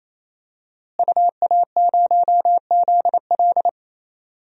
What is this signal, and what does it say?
Morse code, heard as a keyed tone, about 28 wpm, 700 Hz. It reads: UA0ZL
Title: DXCC-100 Call Signs - 5 Characters - Repeated 3x - 28wpm